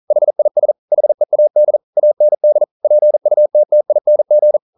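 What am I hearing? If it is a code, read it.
Morse code, heard as a keyed tone, about 42 wpm, 600 Hz. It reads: HIS HEAD AND PUTTING
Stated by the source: Morse code practice at 41 wpm, from CW med Gnister.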